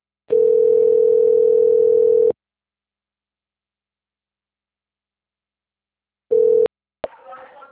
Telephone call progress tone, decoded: ringback tone

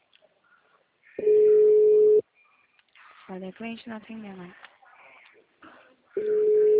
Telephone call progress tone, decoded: ringback tone

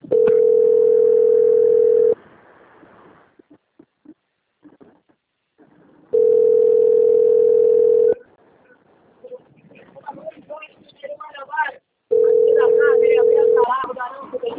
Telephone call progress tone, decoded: ringback tone

